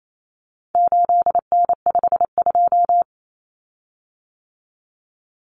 Morse code, heard as a keyed tone, about 28 wpm, 700 Hz. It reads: 8N52
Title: Common Cut Numbers - 4-Digit 28wpm